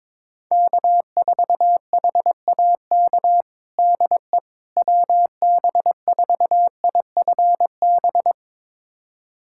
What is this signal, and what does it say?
Morse code, heard as a keyed tone, about 22 wpm, 700 Hz. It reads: K4HAK DE WB4IFB
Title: USA Callsign DE Callsign 22wpm